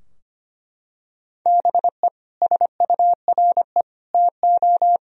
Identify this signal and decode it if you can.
Morse code, heard as a keyed tone, about 25 wpm, 700 Hz. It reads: BE SURE TO
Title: Sentences from Top 1000 Words - Review Part 4 - 25wpm